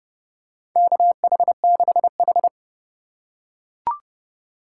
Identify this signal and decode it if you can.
Morse code, heard as a keyed tone, about 28 wpm, 700 Hz. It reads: KH6H
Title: USA Call Signs - 4 Characters - 4x Character Spacing - 30wpm